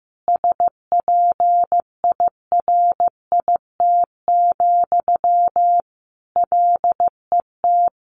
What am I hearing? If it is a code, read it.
Morse code, heard as a keyed tone, about 15 wpm, 700 Hz. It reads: SPIRIT, LET
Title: Sentences from Top 1400 Words 15wpm